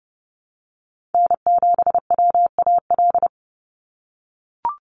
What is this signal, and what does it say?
Morse code, heard as a keyed tone, about 30 wpm, 700 Hz. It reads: N7WAL
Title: USA Call Signs 30wpm